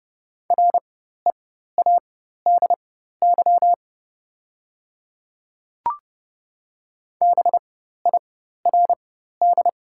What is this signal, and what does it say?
Morse code, heard as a keyed tone, about 30 wpm, spaced as for 14 wpm, 700 Hz. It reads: READY BIRD
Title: 4x Character Spacing - Top 500 Words - 30wpm